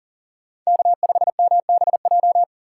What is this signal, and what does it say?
Morse code, heard as a keyed tone, about 39 wpm, 700 Hz. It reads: K5MBJ
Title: USA Call Signs - 5 Characters - 6x Character Spacing - 40wpm